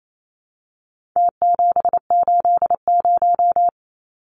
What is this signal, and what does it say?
Morse code, heard as a keyed tone, about 28 wpm, 700 Hz. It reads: T780
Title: Numbers - Contest Style - 4-Digit 28wpm